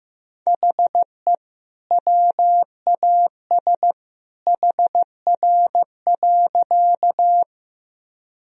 Morse code, heard as a keyed tone, about 15 wpm, 700 Hz. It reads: HE WAS HR.
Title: Sentences from Top 200 Words and Abbreviations 15wpm